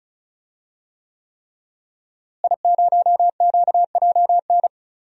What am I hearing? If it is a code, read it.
Morse code, heard as a keyed tone, about 34 wpm, 700 Hz. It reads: I0QJN
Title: DXCC-100 Call Signs - 5 Characters - Repeated 3x - 35wpm